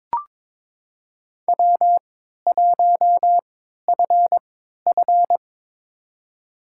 Morse code, heard as a keyed tone, about 22 wpm, spaced as for 12 wpm, 700 Hz. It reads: W1FF
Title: USA Call Signs - 4 Characters - 3x Character Spacing - 22wpm